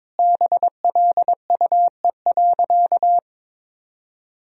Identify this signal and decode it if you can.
Morse code, heard as a keyed tone, about 22 wpm, 700 Hz. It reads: BLUE.
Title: Sentences from Top 1000 Words - Review Part 3 - 22wpm